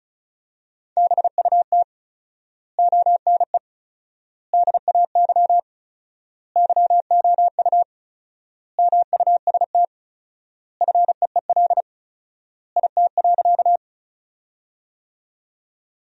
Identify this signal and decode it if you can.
Morse code, heard as a keyed tone, about 36 wpm, 700 Hz. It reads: BUT ONE DAY YOU MUST FEEL IT.
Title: 4x Word-Spacing - Sentences from Top 400 Words - 35wpm